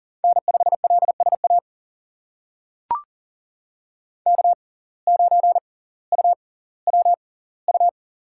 Morse code, heard as a keyed tone, about 40 wpm, spaced as for 20 wpm, 700 Hz. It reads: N5LSA K9UWU
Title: USA Call Signs - 5 Characters - 6x Character Spacing - 40wpm